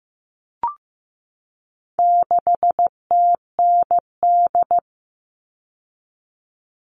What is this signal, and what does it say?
Morse code, heard as a keyed tone, about 15 wpm, 700 Hz. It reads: 6TND
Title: All Cut Numbers - 4-Digit 15wpm